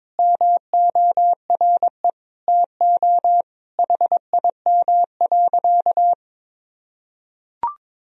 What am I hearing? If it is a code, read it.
Morse code, heard as a keyed tone, about 22 wpm, 700 Hz. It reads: MORE TO HIM.
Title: Sentences from Top 1000 Words - Review Part 1 - 22wpm